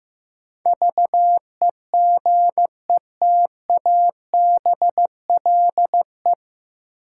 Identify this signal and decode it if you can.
Morse code, heard as a keyed tone, about 16 wpm, 700 Hz. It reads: VEGETABLE
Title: Sentences from Top 1300 Words - New Word Drill - 15wpm